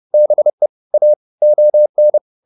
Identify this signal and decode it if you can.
Morse code, heard as a keyed tone, about 30 wpm, 600 Hz. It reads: BE A ON